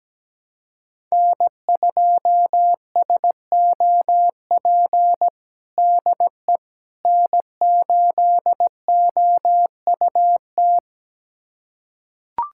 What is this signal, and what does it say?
Morse code, heard as a keyed tone, about 17 wpm, 700 Hz. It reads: N2SOP DE N8OUT
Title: USA Callsign DE Callsign - 5 Characters - 17wpm